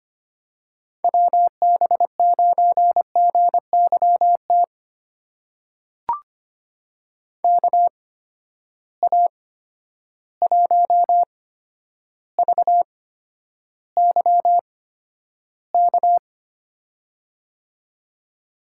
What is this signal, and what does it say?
Morse code, heard as a keyed tone, about 25 wpm, spaced as for 11 wpm, 700 Hz. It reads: WB9GYT KA1VYK
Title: USA Call Signs - 6 Characters - 8x Character Spacing - 25wpm